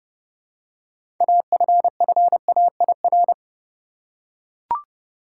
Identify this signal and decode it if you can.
Morse code, heard as a keyed tone, about 32 wpm, 700 Hz. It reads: AFFAIR
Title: Six Letter Words - 30wpm